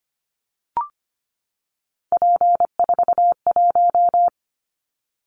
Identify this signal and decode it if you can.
Morse code, heard as a keyed tone, about 25 wpm, 700 Hz. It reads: P41
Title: Callsign Prefixes for Top 100 DXCC - 25wpm